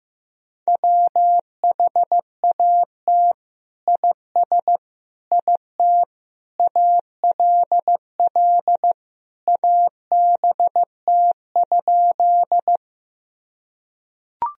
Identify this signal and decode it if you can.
Morse code, heard as a keyed tone, about 15 wpm, 700 Hz. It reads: WHAT IS IT ALL ABT?
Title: Sentences from Top 100 Words and Abbreviations 15wpm